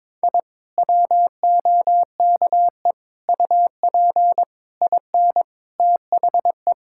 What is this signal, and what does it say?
Morse code, heard as a keyed tone, about 22 wpm, 700 Hz. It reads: I WOKE UP IN THE
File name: Sentences with 7-Word Phrases - 22wpm